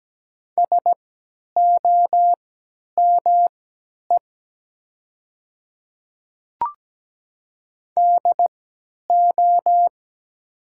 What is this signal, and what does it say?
Morse code, heard as a keyed tone, about 17 wpm, spaced as for 10 wpm, 700 Hz. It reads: SOME DO
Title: 3x Character Spacing - Top 100 Words - 17wpm